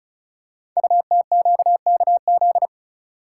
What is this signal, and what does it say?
Morse code, heard as a keyed tone, about 36 wpm, 690 Hz. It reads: UTQKZ
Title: Code Groups - 5 Letters - 35wpm